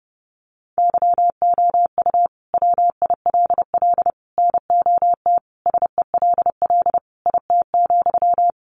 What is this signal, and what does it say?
Morse code, heard as a keyed tone, about 30 wpm, 700 Hz. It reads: YOU WILL NOT SELL IT,
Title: Sentences from Top 1100 Words 30wpm